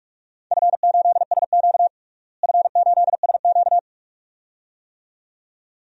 Morse code, heard as a keyed tone, about 44 wpm, 700 Hz. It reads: F8SQ F8SQ
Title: DXCC-100 Call Signs - 4 Characters - Repeated 2x - 45wpm